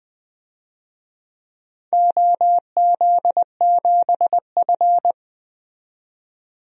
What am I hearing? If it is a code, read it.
Morse code, heard as a keyed tone, about 20 wpm, 700 Hz. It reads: OZ7F
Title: DXCC-100 Call Signs - 4 Characters - 20wpm